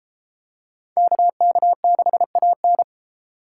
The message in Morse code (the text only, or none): KK6AN